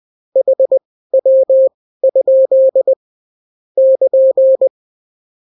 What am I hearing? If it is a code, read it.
Morse code, heard as a keyed tone, about 20 wpm, spaced as for 14 wpm, 530 Hz. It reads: HW? <KN>
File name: CW QSO libroCW4 20-10ppm